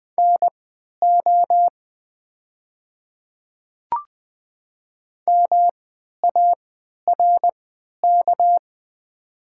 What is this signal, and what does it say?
Morse code, heard as a keyed tone, about 20 wpm, spaced as for 11 wpm, 700 Hz. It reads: NO MARK
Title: 3x Character Spacing - Top 300 Words - 20wpm